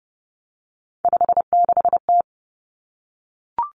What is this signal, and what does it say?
Morse code, heard as a keyed tone, about 30 wpm, 700 Hz. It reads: 56T
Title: Three Challenging Letter-Number Combinations 30wpm